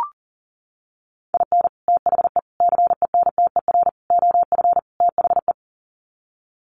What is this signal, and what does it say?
Morse code, heard as a keyed tone, about 40 wpm, 700 Hz. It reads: IN THE CENTER OF THE
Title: Sets of 5 Words 40wpm